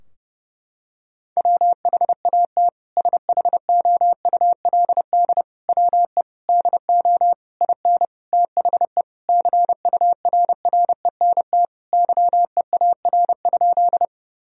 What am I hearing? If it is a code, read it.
Morse code, heard as a keyed tone, about 30 wpm, 700 Hz. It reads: WHAT SHOULD WE DO IN THE CURRENT YEAR?
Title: Sentences from Top 1000 Words - Review Part 4 - 30wpm